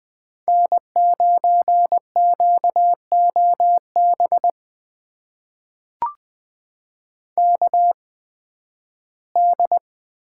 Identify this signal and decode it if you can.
Morse code, heard as a keyed tone, about 20 wpm, spaced as for 13 wpm, 700 Hz. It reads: N9QOB KD
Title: USA Call Signs - 5 Characters - 8x Character Spacing - 20wpm